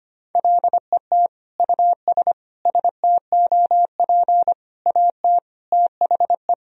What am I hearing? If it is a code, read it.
Morse code, heard as a keyed tone, about 25 wpm, 700 Hz. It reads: LET US STOP AT THE
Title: Sentences from Top 400 Words and Abbreviations 25wpm